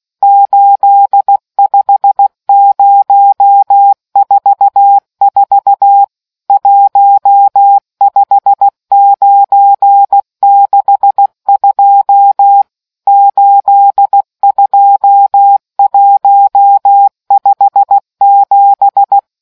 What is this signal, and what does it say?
Morse code, heard as a keyed tone, about 16 wpm, 800 Hz. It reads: 85044 15962 82157